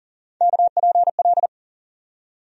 Morse code, heard as a keyed tone, about 40 wpm, 700 Hz. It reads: KPL
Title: USA Call Sign Postfixes - 3 Letters 40wpm